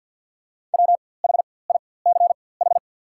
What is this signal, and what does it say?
Morse code, heard as a keyed tone, about 48 wpm, spaced as for 23 wpm, 690 Hz. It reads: WHICH
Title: 4x Character Spacing - Top 100 Words - 50wpm